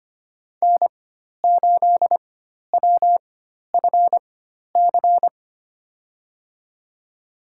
Morse code, heard as a keyed tone, about 25 wpm, spaced as for 12 wpm, 700 Hz. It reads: N8WFC
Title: USA Call Signs - 5 Characters - 4x Character Spacing - 25wpm